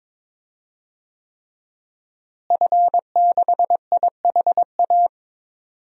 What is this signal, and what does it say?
Morse code, heard as a keyed tone, about 22 wpm, 700 Hz. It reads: F6IHA
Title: DXCC-100 Call Signs - 5 Characters - Repeated 3x - 22wpm